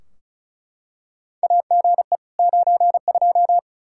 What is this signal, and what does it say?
Morse code, heard as a keyed tone, about 34 wpm, 700 Hz. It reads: AGE 92
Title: Practice QSO - Exchange 3 - More Info - 35wpm